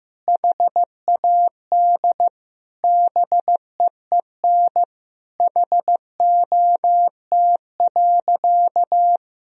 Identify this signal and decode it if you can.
Morse code, heard as a keyed tone, about 15 wpm, 700 Hz. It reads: HAD BEEN HOT.
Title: Sentences from Top 1000 Words - Review Part 3 - 15wpm